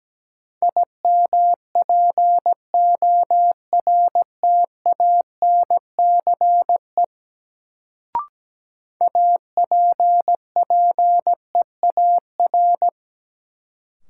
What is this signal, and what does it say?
Morse code, heard as a keyed tone, about 17 wpm, 700 Hz. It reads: IMPORTANCE APPEAR
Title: Sentences from Top 1200 Words - New Word Drill Alt - 17wpm